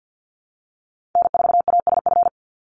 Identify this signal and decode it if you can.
Morse code, heard as a keyed tone, about 50 wpm, 700 Hz. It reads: N4ASF
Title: Call Signs Repeated 3x - 50wpm